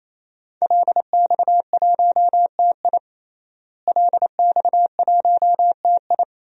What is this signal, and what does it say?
Morse code, heard as a keyed tone, about 28 wpm, 700 Hz. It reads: LX1TI LX1TI
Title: DXCC-100 Call Signs - 5 Characters - Repeated 3x - 28wpm